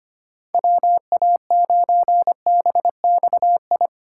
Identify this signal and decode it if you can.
Morse code, heard as a keyed tone, about 25 wpm, 700 Hz. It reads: WA9BXI
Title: USA Call Signs - 6 Characters - 3x Character Spacing - 25wpm